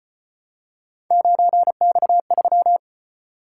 Morse code, heard as a keyed tone, about 35 wpm, 700 Hz. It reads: 9X3